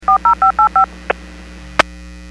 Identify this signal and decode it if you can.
DTMF, heard as telephone touch tones, tones of about 110 ms. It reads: 40585
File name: WPS / 03995 09-59 WPS 451.25 WPS